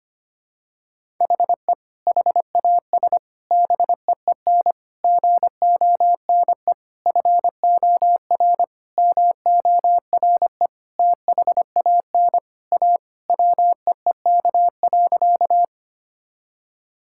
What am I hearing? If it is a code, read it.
Morse code, heard as a keyed tone, about 25 wpm, 700 Hz. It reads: HE HAS BEEN GONE FOR MORE THAN A WEEK.
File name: practice / 25wpm / Sentences with 4-Word Phrases - 25wpm